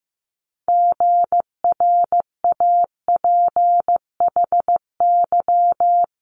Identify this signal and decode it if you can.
Morse code, heard as a keyed tone, about 15 wpm, 700 Hz. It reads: GRAPHY
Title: Multi-letter Phonemes Quick Fire 15wpm